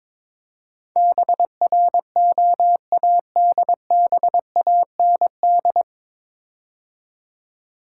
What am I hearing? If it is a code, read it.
Morse code, heard as a keyed tone, about 22 wpm, 700 Hz. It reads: BROADBAND
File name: 9 Letter Words - First 3 Letters Given - 22wpm